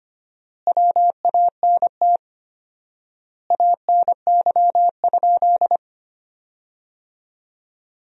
Morse code, heard as a keyed tone, about 25 wpm, 700 Hz. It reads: WANT ANY?
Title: 4x Word-Spacing - Sentences from Top 300 Words - 25wpm